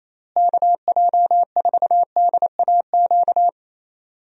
Morse code, heard as a keyed tone, about 27 wpm, 700 Hz. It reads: KJ4DAQ
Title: USA Call Signs - 6 Characters - Repeated 3x - 28wpm